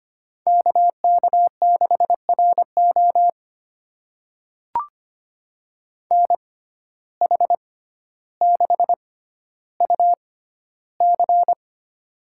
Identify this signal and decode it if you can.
Morse code, heard as a keyed tone, about 25 wpm, spaced as for 13 wpm, 700 Hz. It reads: KK6RO NH6UC
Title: USA Call Signs - 5 Characters - 6x Character Spacing - 25wpm